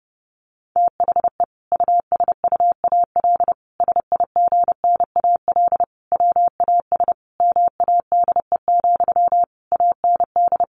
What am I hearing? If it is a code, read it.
Morse code, heard as a keyed tone, about 30 wpm, 700 Hz. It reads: THE USUAL SIGNAL WAS MADE, AND
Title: Sentences from Top 1400 Words 30wpm